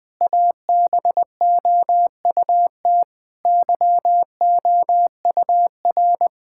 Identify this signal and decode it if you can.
Morse code, heard as a keyed tone, about 20 wpm, 700 Hz. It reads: ABOUT YOUR
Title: Sentences with 4-Word Phrases - 20wpm